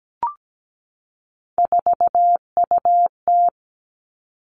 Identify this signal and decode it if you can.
Morse code, heard as a keyed tone, about 17 wpm, 700 Hz. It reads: 4UT
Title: Three Challenging Letter-Number Combinations 17wpm